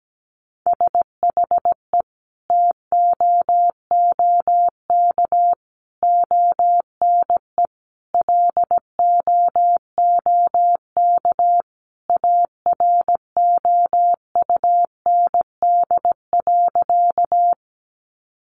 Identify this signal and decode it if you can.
Morse code, heard as a keyed tone, about 17 wpm, 700 Hz. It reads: SHE TOOK ONE LOOK AROUND.
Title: Sentences from Top 600 Words 17wpm